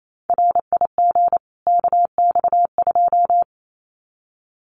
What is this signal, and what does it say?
Morse code, heard as a keyed tone, about 28 wpm, 700 Hz. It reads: RIG KX2
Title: Common QSO Phrases 28wpm